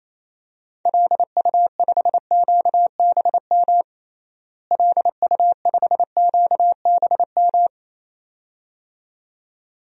Morse code, heard as a keyed tone, about 28 wpm, 700 Hz. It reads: LU5QBM LU5QBM
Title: DXCC-100 Call Signs - 6 Characters - Repeated 2x - 28wpm